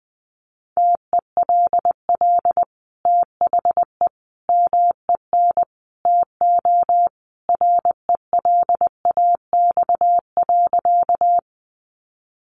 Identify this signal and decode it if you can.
Morse code, heard as a keyed tone, about 20 wpm, 700 Hz. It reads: TELL THE MEN TO RELAX.
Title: Sentences from Top 1400 Words 20wpm